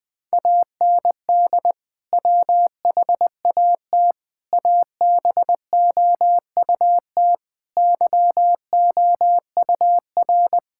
Morse code, heard as a keyed tone, about 20 wpm, 700 Hz. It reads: AND WHAT ABOUT YOUR
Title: Sentences from Top 1000 Words - Review Part 6 - 20wpm